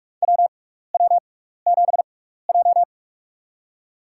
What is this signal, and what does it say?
Morse code, heard as a keyed tone, about 46 wpm, spaced as for 16 wpm, 700 Hz. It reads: WW7J
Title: USA Call Signs - 4 Characters - 6x Character Spacing - 45wpm